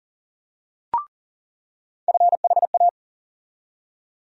Morse code, heard as a keyed tone, about 40 wpm, 700 Hz. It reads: FHA
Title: USA Call Sign Postfixes - 3 Letters 40wpm